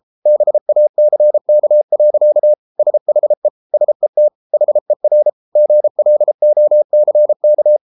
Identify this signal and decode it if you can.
Morse code, heard as a keyed tone, about 33 wpm, 600 Hz. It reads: BACK. SHE SET HER GLOCK